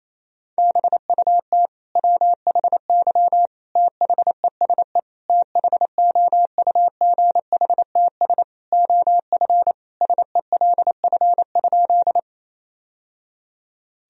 Morse code, heard as a keyed tone, about 28 wpm, 700 Hz. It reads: BUT WHY THESE THOUGHTS OF SELF?
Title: Sentences from Top 1000 Words - Review Part 1 - 28wpm